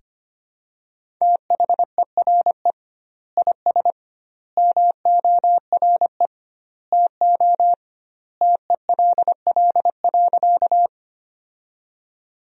Morse code, heard as a keyed tone, about 25 wpm, 700 Hz. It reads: THERE IS MORE TO TELL.
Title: 2x Word-Spacing - Sentences from Top 200 Words - 25wpm